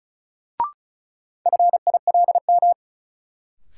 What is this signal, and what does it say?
Morse code, heard as a keyed tone, about 36 wpm, 700 Hz. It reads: FILM
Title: Sentences from Top 1000 Words - Review Part 3 - New Word Drill - 35wpm